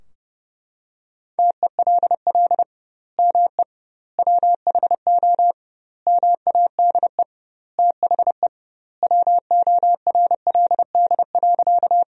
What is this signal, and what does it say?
Morse code, heard as a keyed tone, about 32 wpm, 700 Hz. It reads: TELL ME WHO MADE THE WORLD.
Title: 2x Word-Spacing - Sentences from Top 200 Words - 30wpm